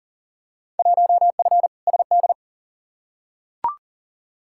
Morse code, heard as a keyed tone, about 40 wpm, 700 Hz. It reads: 1F SD
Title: Field Day Exchanges - 40wpm